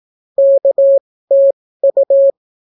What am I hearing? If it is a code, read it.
Morse code, heard as a keyed tone, about 18 wpm, spaced as for 15 wpm, 550 Hz. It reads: KTU